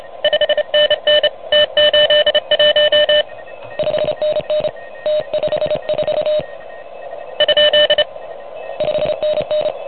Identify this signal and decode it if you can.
Morse code, heard as a keyed tone, about 30 wpm, 600 Hz. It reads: 5NN T81 5NN T54 ? 5NN
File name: truSDX / MWC29.8